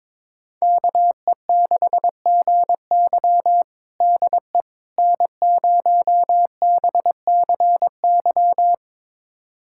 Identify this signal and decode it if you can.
Morse code, heard as a keyed tone, about 22 wpm, 700 Hz. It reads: KE6GY DE N0BCY
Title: USA Callsign DE Callsign 22wpm